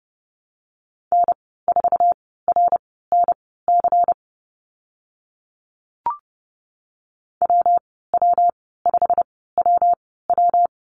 Morse code, heard as a keyed tone, about 30 wpm, spaced as for 17 wpm, 700 Hz. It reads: N4RNC WW5WW
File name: Lesson 19g - Letter w - Call Signs 30wpm